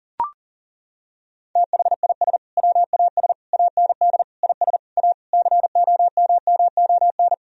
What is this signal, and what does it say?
Morse code, heard as a keyed tone, about 40 wpm, 700 Hz. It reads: THIS WAS AND IS A COMMON
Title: Sentences from Top 1000 Words - Review Part 4 - 40wpm